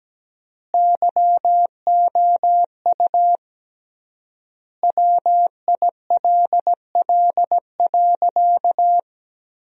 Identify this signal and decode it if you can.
Morse code, heard as a keyed tone, about 17 wpm, 700 Hz. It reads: YOU WILL.
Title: 3x Word-Spacing - Sentences from Top 100 Words - 17wpm